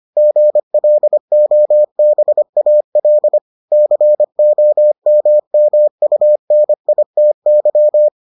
Morse code, heard as a keyed tone, about 25 wpm, 600 Hz. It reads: GLOBAL COMMUNITY